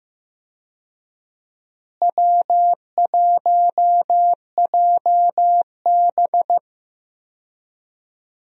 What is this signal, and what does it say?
Morse code, heard as a keyed tone, about 15 wpm, 700 Hz. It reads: W1JB
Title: USA Call Signs - 4 Characters - Repeated 2x - 15wpm